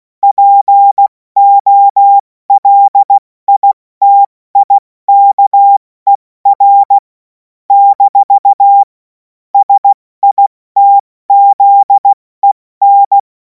Morse code, heard as a keyed tone, about 16 wpm, 800 Hz. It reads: POLITIKER - SITZEN
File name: Morsekurs / Witze / Witz 24. Februar 2026 70 F80